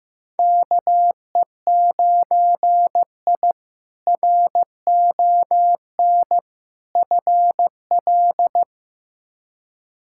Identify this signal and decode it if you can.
Morse code, heard as a keyed tone, about 15 wpm, 700 Hz. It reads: KE9I RON FL
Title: K1USN SST QSO Practice - 15wpm